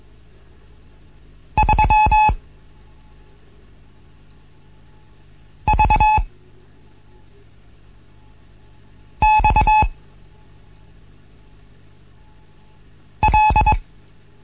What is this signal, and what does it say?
Morse code, heard as a keyed tone, about 22 wpm, 860 Hz. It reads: 3 V X L
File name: Morsen / Kurs A 15